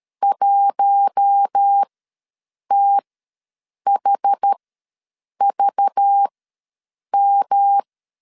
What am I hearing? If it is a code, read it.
Morse code, heard as a keyed tone, about 13 wpm, spaced as for 7 wpm, 800 Hz. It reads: 1THVM
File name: LEZIONI / 5 LBVC / 5 LBV ws 12wpm 3015